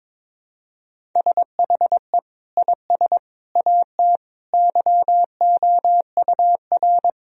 Morse code, heard as a keyed tone, about 21 wpm, 700 Hz. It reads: SHE IS AT YOUR
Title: Sentences from Top 1000 Words - Review Part 1 - 22wpm